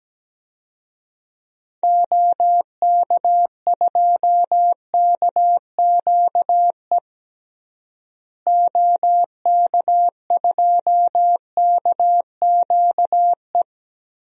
Morse code, heard as a keyed tone, about 17 wpm, 700 Hz. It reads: OK2KQE OK2KQE
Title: DXCC-100 Call Signs - 6 Characters - Repeated 2x - 17wpm